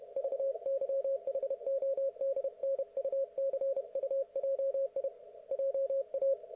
Morse code, heard as a keyed tone, about 31 wpm, 530 Hz. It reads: VYHODNUCUJI JA
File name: Archiv / CUC107 / ok1fhi cul107